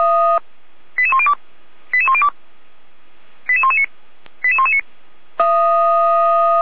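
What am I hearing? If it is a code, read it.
Five-tone selective call (ZVEI), heard as quick runs of five tones, tones of about 70 ms. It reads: 88102 88102 88119 88119
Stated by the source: Five-tone ZVEI selective call received on an SDR, recorded by Feuerwehr Orlamünde.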